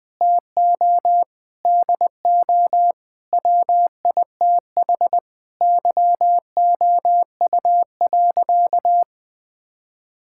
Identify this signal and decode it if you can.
Morse code, heard as a keyed tone, about 20 wpm, 700 Hz. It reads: TO DO WITH YOU.
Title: Sentences with 5-Word Phrases - 20wpm